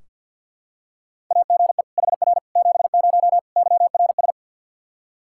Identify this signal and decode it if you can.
Morse code, heard as a keyed tone, about 45 wpm, 700 Hz. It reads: AGE HR 70 YRS
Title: Practice QSO - Exchange 3 - More Info - 50wpm